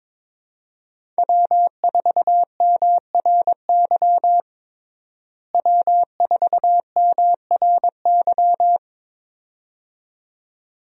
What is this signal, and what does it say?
Morse code, heard as a keyed tone, about 21 wpm, 700 Hz. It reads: W4MRY W4MRY
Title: USA Call Signs - 5 Characters - Repeated 2x - 22wpm